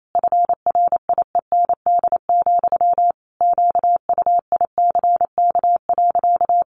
Morse code, heard as a keyed tone, about 28 wpm, 700 Hz. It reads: FRIEND, QUICK.
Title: Sentences from Top 500 Words 28wpm